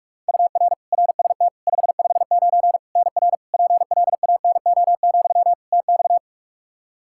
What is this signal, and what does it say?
Morse code, heard as a keyed tone, about 41 wpm, 700 Hz. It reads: UR RST 559 NR PLANO, TX
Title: Practice QSO - Exchange 1 - Beginning - 45wpm